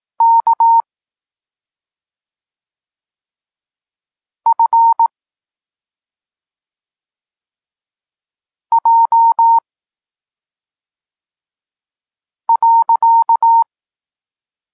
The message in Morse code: K F J .